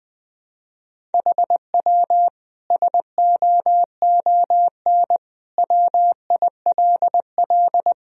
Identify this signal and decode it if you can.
Morse code, heard as a keyed tone, about 20 wpm, 700 Hz. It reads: HW SOON WILL
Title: Sentences from Top 400 Words and Abbreviations 20wpm